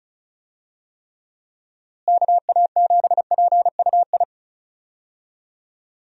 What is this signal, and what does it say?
Morse code, heard as a keyed tone, about 36 wpm, 700 Hz. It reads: KA7PUI
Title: USA Call Signs - 6 Characters - Repeated 3x - 35wpm